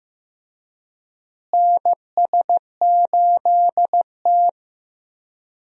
Morse code, heard as a keyed tone, about 15 wpm, 700 Hz. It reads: NS8T
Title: USA Call Signs - 4 Characters - Repeated 2x - 15wpm